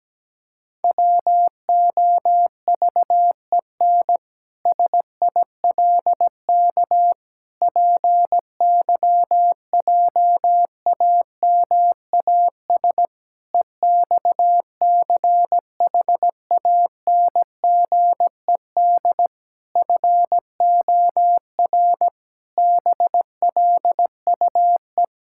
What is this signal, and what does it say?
Morse code, heard as a keyed tone, about 17 wpm, 700 Hz. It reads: WOVEN SILK PYJAMAS EXCHANGED FOR BLUE
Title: Pangram Challenge v2 - 17wpm